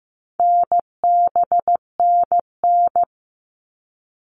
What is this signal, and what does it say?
Morse code, heard as a keyed tone, about 15 wpm, 700 Hz. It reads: NBNN
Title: All Cut Numbers - 4-Digit 15wpm